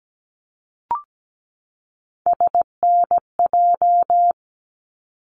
Morse code, heard as a keyed tone, about 17 wpm, 700 Hz. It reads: SNJ
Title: ARRL Sections 17wpm